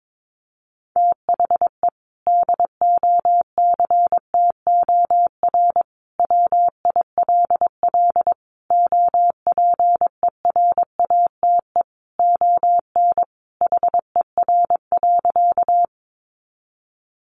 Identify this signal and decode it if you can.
Morse code, heard as a keyed tone, about 22 wpm, 700 Hz. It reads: THE DOCTOR WILL OPERATE ON HER.
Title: Sentences from Top 1500 Words 22wpm